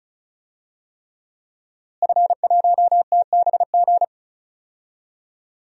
Morse code, heard as a keyed tone, about 34 wpm, 690 Hz. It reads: F1TBG
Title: DXCC-100 Call Signs - 5 Characters - Repeated 3x - 35wpm